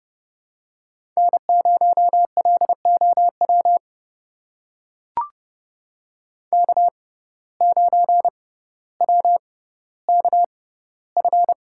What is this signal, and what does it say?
Morse code, heard as a keyed tone, about 30 wpm, spaced as for 15 wpm, 700 Hz. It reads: N0LOW K9WKF
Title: USA Call Signs - 5 Characters - 6x Character Spacing - 30wpm